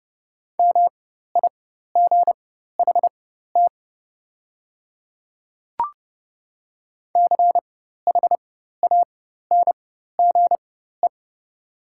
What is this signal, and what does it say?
Morse code, heard as a keyed tone, about 30 wpm, spaced as for 14 wpm, 700 Hz. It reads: MIGHT CHANGE
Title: 4x Character Spacing - Top 300 Words - 30wpm